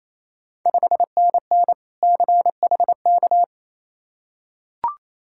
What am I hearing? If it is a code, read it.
Morse code, heard as a keyed tone, about 29 wpm, 690 Hz. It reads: 5NN CHK
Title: Ontario QSO Party Exchange 28wpm